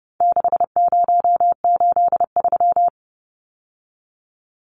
Morse code, heard as a keyed tone, about 28 wpm, 700 Hz. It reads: 6083
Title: Numbers - 4-Digit 30wpm